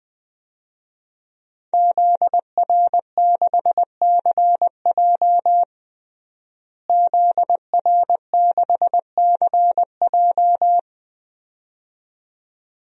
Morse code, heard as a keyed tone, about 21 wpm, 700 Hz. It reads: ZR6CJ ZR6CJ
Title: DXCC-100 Call Signs - 5 Characters - Repeated 2x - 20wpm